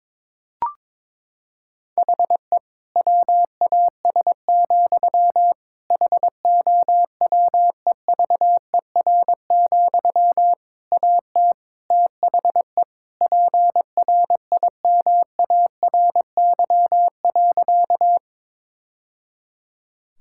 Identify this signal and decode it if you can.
Morse code, heard as a keyed tone, about 22 wpm, 700 Hz. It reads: HE WAS, HOWEVER, AT THE PRIMARY.
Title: Sentences from Top 1000 Words - Review Part 5 - 22wpm